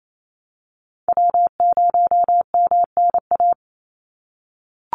Morse code, heard as a keyed tone, about 28 wpm, 700 Hz. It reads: W0MNA
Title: NA SOTA Chaser Callsigns 28wpm